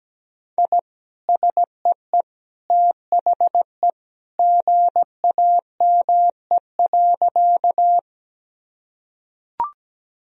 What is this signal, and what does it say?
Morse code, heard as a keyed tone, about 17 wpm, 700 Hz. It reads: I SEE THE GAME.
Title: Sentences from Top 1000 Words - Review Part 3 - 17wpm